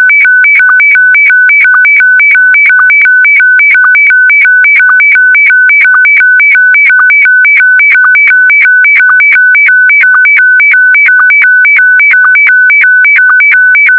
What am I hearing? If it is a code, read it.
SSTV (slow-scan television), Scottie DX.